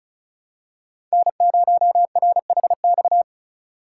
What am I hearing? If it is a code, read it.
Morse code, heard as a keyed tone, about 36 wpm, 700 Hz. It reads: N0RHX
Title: USA Call Signs - 5 Characters - 2x Character Spacing - 35wpm